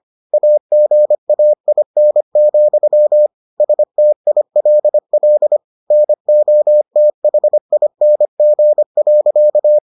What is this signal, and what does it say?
Morse code, heard as a keyed tone, about 25 wpm, 600 Hz. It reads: AGAIN, STILL NOTHING.